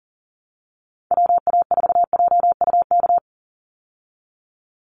Morse code, heard as a keyed tone, about 40 wpm, 700 Hz. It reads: WA4JUK
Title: Call Signs Repeated 3x - 40wpm